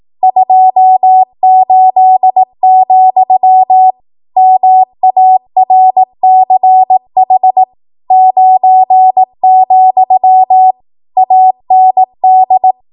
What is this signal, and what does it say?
Morse code, heard as a keyed tone, about 18 wpm, 750 Hz. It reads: 28, MARCH 9, AND